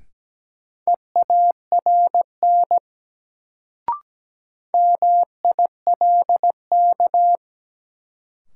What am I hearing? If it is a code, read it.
Morse code, heard as a keyed tone, about 17 wpm, 700 Hz. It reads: EARN MILK
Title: Sentences from Top 1500 Words - New Word Drill Alt - 17wpm